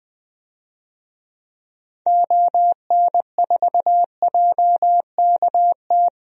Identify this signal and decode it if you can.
Morse code, heard as a keyed tone, about 20 wpm, 700 Hz. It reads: ON4JKT
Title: DXCC-100 Call Signs - 6 Characters - Repeated 2x - 20wpm